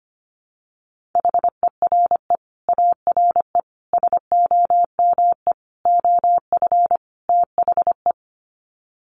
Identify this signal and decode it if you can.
Morse code, heard as a keyed tone, about 25 wpm, 700 Hz. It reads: HERE ARE SOME OF THE
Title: Sets of 5 Words Encore 25wpm